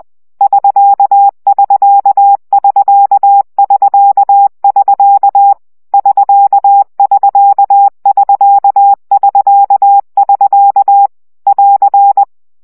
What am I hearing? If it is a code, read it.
Morse code, heard as a keyed tone, about 20 wpm, 800 Hz. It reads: <SK><SK><SK><SK><SK> <SK><SK><SK><SK><SK> <AR>
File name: CW gyakorlas sk 17 baud